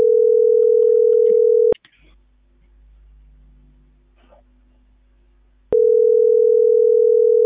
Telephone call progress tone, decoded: ringback tone

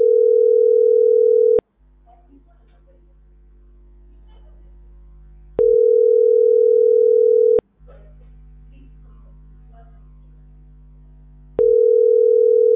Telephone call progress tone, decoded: ringback tone

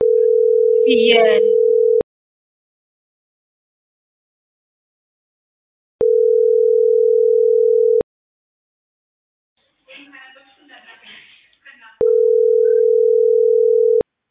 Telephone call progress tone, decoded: ringback tone

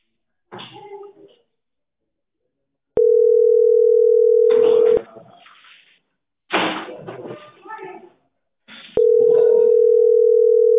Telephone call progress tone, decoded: ringback tone

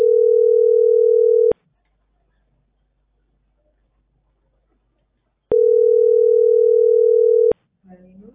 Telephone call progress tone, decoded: ringback tone

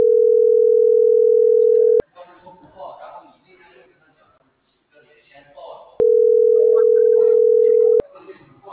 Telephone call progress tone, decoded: ringback tone

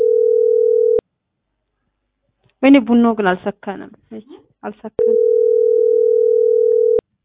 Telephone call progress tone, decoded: ringback tone

